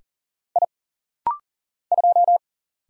Morse code, heard as a keyed tone, about 40 wpm, 700 Hz. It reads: I 2
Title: Single Letter-Number Alt - Rapid-Fire 40wpm